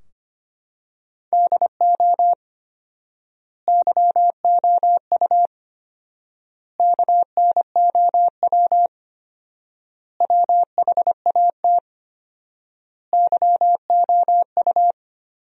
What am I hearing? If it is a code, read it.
Morse code, heard as a keyed tone, about 25 wpm, 700 Hz. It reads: DO YOU KNOW WHAT YOU
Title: 4x Word-Spacing - Sentences from Top 100 Words - 25wpm